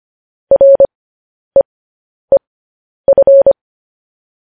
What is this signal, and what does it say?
Morse code, heard as a keyed tone, about 25 wpm, spaced as for 10 wpm, 550 Hz. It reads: REEF
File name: podcast / News headlines-1x-25x10